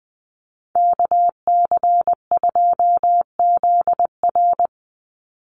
Morse code, heard as a keyed tone, about 20 wpm, 700 Hz. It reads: KC2ZR
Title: USA Call Signs 20wpm